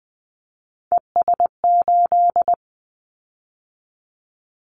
Morse code, heard as a keyed tone, about 20 wpm, 700 Hz. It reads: ES8
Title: Callsign Prefixes for Top 50 DXCC - 20wpm